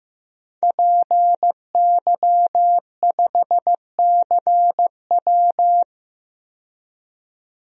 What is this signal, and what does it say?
Morse code, heard as a keyed tone, about 15 wpm, 700 Hz. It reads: PY5CW
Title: DXCC-100 Call Signs - 5 Characters - 15wpm